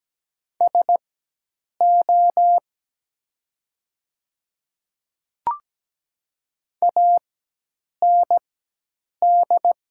Morse code, heard as a keyed tone, about 17 wpm, spaced as for 8 wpm, 700 Hz. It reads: SO AND
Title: 4x Character Spacing - Top 100 Words - 17wpm